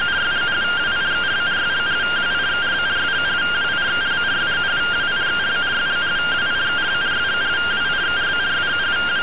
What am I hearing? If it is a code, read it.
Two-tone teleprinter signal, about 170 Hz shift.